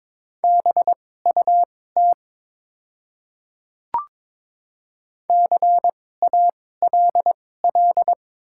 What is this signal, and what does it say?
Morse code, heard as a keyed tone, about 22 wpm, spaced as for 16 wpm, 700 Hz. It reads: BUT CALL
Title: 2x Character Spacing - Top 100 Words - 22wpm